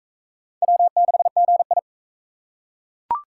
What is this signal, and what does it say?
Morse code, heard as a keyed tone, about 38 wpm, 700 Hz. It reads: W6GI